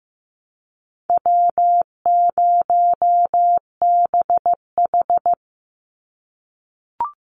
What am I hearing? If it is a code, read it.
Morse code, heard as a keyed tone, about 15 wpm, 700 Hz. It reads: W0BH
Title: Easy Call Signs - 15wpm